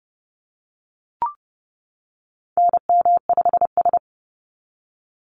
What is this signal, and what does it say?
Morse code, heard as a keyed tone, about 30 wpm, 700 Hz. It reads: NM5S
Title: NA SOTA Activator Callsigns 30wpm